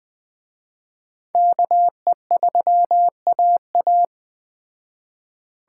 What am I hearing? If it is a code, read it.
Morse code, heard as a keyed tone, about 21 wpm, 700 Hz. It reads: KE3AA